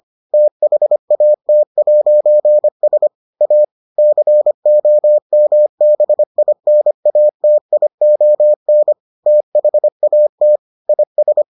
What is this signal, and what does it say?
Morse code, heard as a keyed tone, about 25 wpm, 600 Hz. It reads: THAT'S A COMBINATION THAT IS